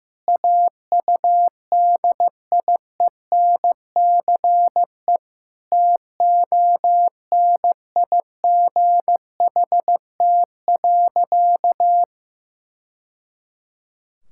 Morse code, heard as a keyed tone, about 15 wpm, 700 Hz. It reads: AUDIENCE TONIGHT.
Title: Sentences from Top 1000 Words - Review Part 5 - 15wpm